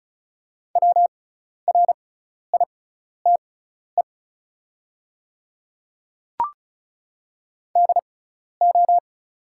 Morse code, heard as a keyed tone, about 36 wpm, spaced as for 12 wpm, 700 Hz. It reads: WRITE DO
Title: 6x Character Spacing - Top 100 Words - 35wpm